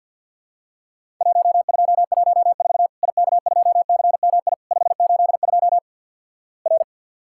Morse code, heard as a keyed tone, about 50 wpm, 700 Hz. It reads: 1214 IK2CMI 582 R
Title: Worked All Europe DX Contest - QTC Exchange with Repeats - 50wpm